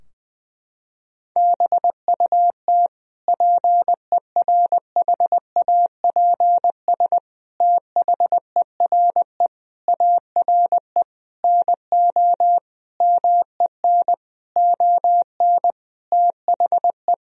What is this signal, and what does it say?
Morse code, heard as a keyed tone, about 20 wpm, 700 Hz. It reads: BUT PERHAPS THERE ARE NO MEN ON THE
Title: Sentences from Top 1000 Words - Review Part 3 - 20wpm